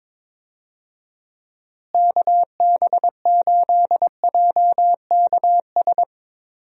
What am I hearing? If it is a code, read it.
Morse code, heard as a keyed tone, about 22 wpm, 700 Hz. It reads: KB8JKS
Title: USA Call Signs - 6 Characters - Repeated 2x - 22wpm